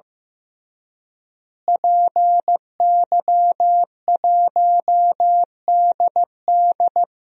Morse code, heard as a keyed tone, about 15 wpm, 700 Hz. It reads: PY1DD
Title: DXCC-100 Call Signs - 5 Characters - Repeated 3x - 15wpm